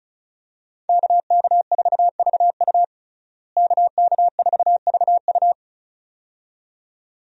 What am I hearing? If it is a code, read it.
Morse code, heard as a keyed tone, about 36 wpm, 700 Hz. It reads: KK4VU KK4VU
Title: USA Call Signs - 5 Characters - Repeated 2x - 35wpm